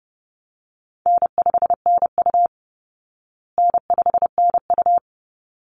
Morse code, heard as a keyed tone, about 30 wpm, 700 Hz. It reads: N5NU N5NU
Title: Call Signs Repeated 3x - 30wpm